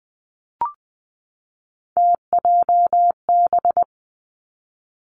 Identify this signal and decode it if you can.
Morse code, heard as a keyed tone, about 20 wpm, 700 Hz. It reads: TJB
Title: Three Challenging Letter-Number Combinations 20wpm